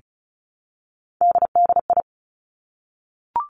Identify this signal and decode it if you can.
Morse code, heard as a keyed tone, about 36 wpm, 700 Hz. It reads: DDI
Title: Three Challenging Letter-Number Combinations 35wpm